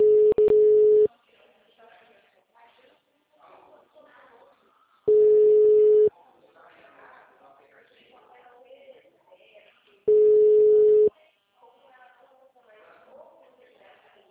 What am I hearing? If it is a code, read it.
Ringback tone (Europe, 425 Hz), with speech.